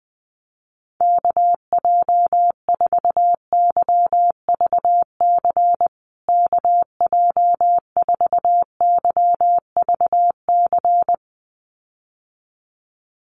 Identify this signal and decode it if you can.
Morse code, heard as a keyed tone, about 20 wpm, 700 Hz. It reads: KJ4YVC KJ4YVC
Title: USA Call Signs Repeated 2x 20wpm